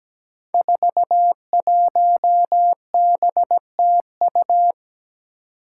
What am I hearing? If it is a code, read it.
Morse code, heard as a keyed tone, about 17 wpm, 700 Hz. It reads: 41BTU
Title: Five Challenging Letter-Number Combinations 17wpm